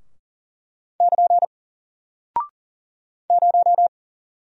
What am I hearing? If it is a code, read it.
Morse code, heard as a keyed tone, about 40 wpm, 700 Hz. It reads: <KN> 0
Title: Single Character (Letter:Number:Punctuation:Prosign) Alt 40wpm